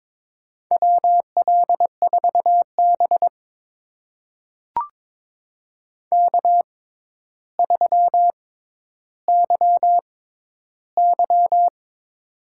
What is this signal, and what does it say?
Morse code, heard as a keyed tone, about 22 wpm, spaced as for 11 wpm, 700 Hz. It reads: WL4B K3YY
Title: USA Call Signs - 4 Characters - 6x Character Spacing - 22wpm